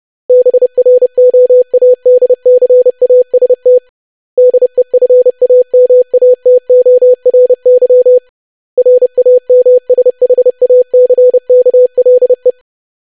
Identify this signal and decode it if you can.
Morse code, heard as a keyed tone, about 30 wpm, 500 Hz. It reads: BROADCAST DEFAMATORY RAMSHACKLE
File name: lesson / web / longwords-25wpm